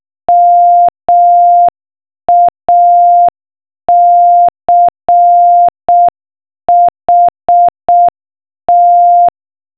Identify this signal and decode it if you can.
Morse code, heard as a keyed tone, about 6 wpm, 700 Hz. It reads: MACHT
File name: morsecode-knacken